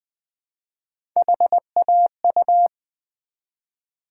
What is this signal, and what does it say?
Morse code, heard as a keyed tone, about 20 wpm, 700 Hz. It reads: HAU
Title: Build A Word - 1 Letter At A Time - 5-Letter Words - 20wpm